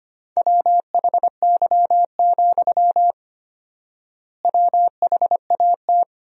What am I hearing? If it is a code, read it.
Morse code, heard as a keyed tone, about 25 wpm, 700 Hz. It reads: WHY, WHAT
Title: 4x Word-Spacing - Sentences from Top 200 Words - 25wpm